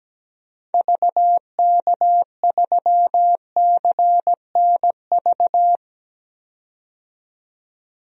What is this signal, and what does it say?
Morse code, heard as a keyed tone, about 17 wpm, 700 Hz. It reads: VK3CNV
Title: DXCC-100 Call Signs - 6 Characters - 17wpm